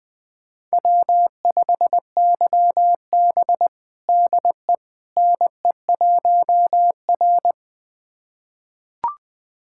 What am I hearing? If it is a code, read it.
Morse code, heard as a keyed tone, about 20 wpm, 700 Hz. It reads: W5YB DE NE1R
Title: USA Callsign DE Callsign - 4 Characters - 20wpm